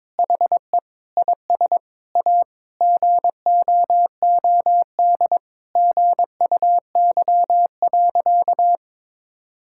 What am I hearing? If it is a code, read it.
Morse code, heard as a keyed tone, about 22 wpm, 700 Hz. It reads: HE IS A GOOD GUY.
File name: practice / 22wpm / Sentences with 8-Word Phrases - 22wpm